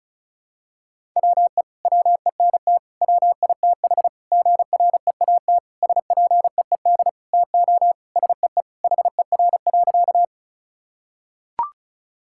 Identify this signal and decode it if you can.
Morse code, heard as a keyed tone, about 34 wpm, 700 Hz. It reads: WE WENT WITH GREAT SPEED TO SEE HER.
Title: Sentences from Top 1000 Words - Review Part 5 - 35wpm